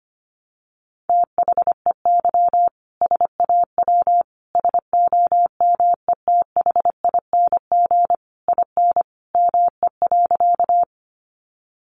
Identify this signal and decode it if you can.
Morse code, heard as a keyed tone, about 25 wpm, 700 Hz. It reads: THEY SAW SOMETHING IN ME.
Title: Sentences from Top 600 Words 25wpm